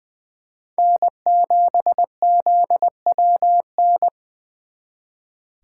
Morse code, heard as a keyed tone, about 23 wpm, 700 Hz. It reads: N7ZWN
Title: Speed Racing - USA 1x3 Call Signs 20wpm